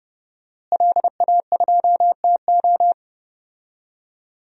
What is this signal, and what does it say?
Morse code, heard as a keyed tone, about 30 wpm, 700 Hz. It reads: LA2TO
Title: DXCC-100 Call Signs - 5 Characters - Repeated 2x - 30wpm